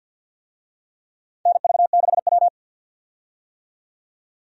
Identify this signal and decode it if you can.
Morse code, heard as a keyed tone, about 51 wpm, 700 Hz. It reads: NV6W
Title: USA Call Signs - 4 Characters - 50wpm